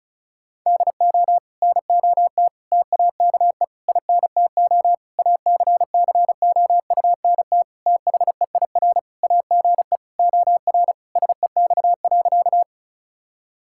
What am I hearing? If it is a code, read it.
Morse code, heard as a keyed tone, about 34 wpm, 700 Hz. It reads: DO NOT TAKE INTO ACCOUNT THEIR AGE OR SEX.
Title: Sentences from Top 1600 Words 35wpm